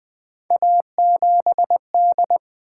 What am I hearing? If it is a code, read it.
Morse code, heard as a keyed tone, about 21 wpm, 700 Hz. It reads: A7D
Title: Three Letter-Number Combinations - 20wpm